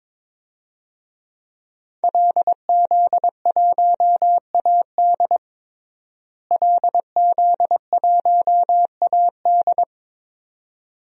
Morse code, heard as a keyed tone, about 22 wpm, 700 Hz. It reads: LZ1AD LZ1AD
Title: DXCC-100 Call Signs - 5 Characters - Repeated 2x - 22wpm